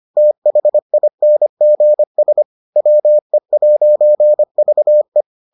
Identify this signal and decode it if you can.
Morse code, heard as a keyed tone, about 25 wpm, 600 Hz. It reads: THINGS WE'VE